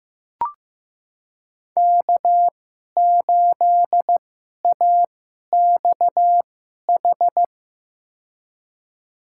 Morse code, heard as a keyed tone, about 15 wpm, spaced as for 11 wpm, 700 Hz. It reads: K8AXH
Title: USA Call Signs - 5 Characters - 2x Character Spacing - 15wpm